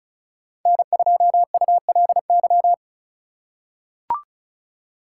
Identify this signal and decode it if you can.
Morse code, heard as a keyed tone, about 34 wpm, 700 Hz. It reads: N2ULY
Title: USA Call Signs - 5 Characters - 3x Character Spacing - 35wpm